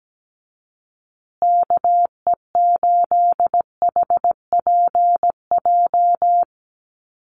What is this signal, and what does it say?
Morse code, heard as a keyed tone, about 17 wpm, 700 Hz. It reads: KE8HPJ
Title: USA Call Signs Repeated 3x 17wpm